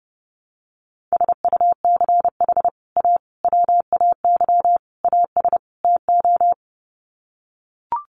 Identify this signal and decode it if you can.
Morse code, heard as a keyed tone, about 30 wpm, 700 Hz. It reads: SUCH A WAY AS TO
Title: Sets of 5 Words Encore2 30wpm